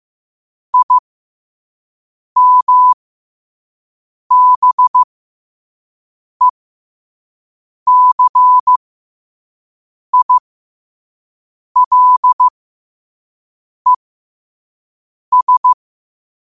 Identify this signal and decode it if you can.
Morse code, heard as a keyed tone, about 15 wpm, spaced as for 5 wpm, 1000 Hz. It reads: IMBECILES
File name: qotdfiles / 5wpm